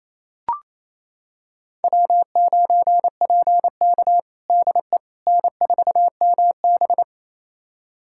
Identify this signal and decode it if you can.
Morse code, heard as a keyed tone, about 28 wpm, 700 Hz. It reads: W9PK DE N4MB
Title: USA Callsign DE Callsign - 4 Characters - 28wpm